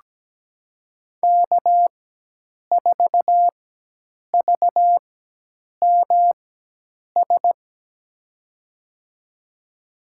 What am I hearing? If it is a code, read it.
Morse code, heard as a keyed tone, about 17 wpm, spaced as for 8 wpm, 700 Hz. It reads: K4VMS
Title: USA Call Signs - 4x Character Spacing - 17wpm